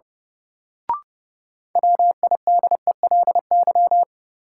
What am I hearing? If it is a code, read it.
Morse code, heard as a keyed tone, about 32 wpm, 700 Hz. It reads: WIDELY
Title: Sentences from Top 1200 Words - New Word Drill Alt - 30wpm